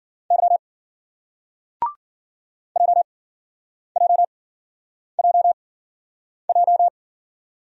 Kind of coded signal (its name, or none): Morse code